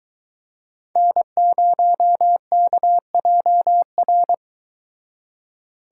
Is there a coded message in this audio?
Yes: Morse code from 1.0 to 4.3 s, heard as a keyed tone, about 24 wpm, 700 Hz. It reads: N0KJR